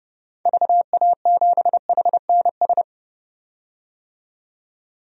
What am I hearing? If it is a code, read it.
Morse code, heard as a keyed tone, about 32 wpm, 700 Hz. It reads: VA7HNS
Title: DXCC-100 Call Signs - 6 Characters - 30wpm